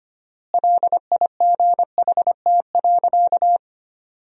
Morse code, heard as a keyed tone, about 25 wpm, 700 Hz. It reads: LIGHT.
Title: 4x Word-Spacing - Sentences from Top 400 Words - 25wpm